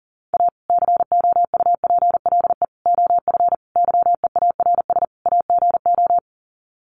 Morse code, heard as a keyed tone, about 40 wpm, 700 Hz. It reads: A COUPLE OF YEARS AGO
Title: Sets of 5 Words 40wpm